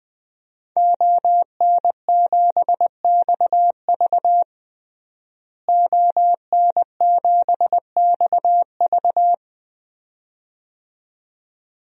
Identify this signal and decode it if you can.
Morse code, heard as a keyed tone, about 20 wpm, 700 Hz. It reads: ON7XV ON7XV
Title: DXCC-100 Call Signs - 5 Characters - Repeated 2x - 20wpm